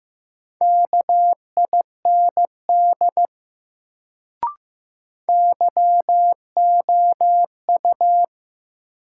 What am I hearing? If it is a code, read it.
Morse code, heard as a keyed tone, about 15 wpm, 700 Hz. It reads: KIND YOU
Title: Sets of 2 Words - New Word Drill - 15wpm